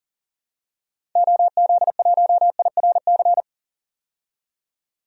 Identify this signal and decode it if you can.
Morse code, heard as a keyed tone, about 40 wpm, 700 Hz. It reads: OZ1IRC
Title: DXCC-100 Call Signs - 6 Characters - 40wpm